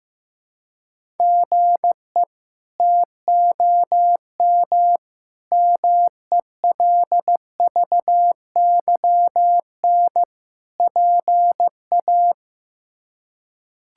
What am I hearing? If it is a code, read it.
Morse code, heard as a keyed tone, about 15 wpm, 700 Hz. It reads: GE TOM MELVYN PA
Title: K1USN SST QSO Practice - 15wpm